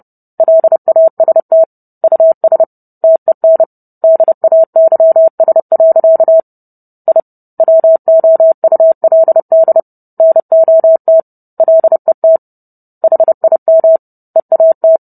Morse code, heard as a keyed tone, about 30 wpm, 650 Hz. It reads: LAST US TEN DAYS. I WOULD NOT LET HIM EAT